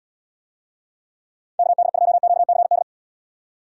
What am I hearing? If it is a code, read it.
Morse code, heard as a keyed tone, about 70 wpm, 700 Hz. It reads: KD2QCD